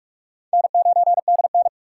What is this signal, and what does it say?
Morse code, heard as a keyed tone, about 45 wpm, 700 Hz. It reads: N9DN
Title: USA Call Signs - 4 Characters - 6x Character Spacing - 45wpm